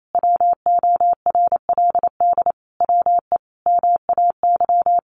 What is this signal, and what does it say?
Morse code, heard as a keyed tone, about 28 wpm, 700 Hz. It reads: WORLD WE MAY
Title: Sentences from Top 1100 Words 28wpm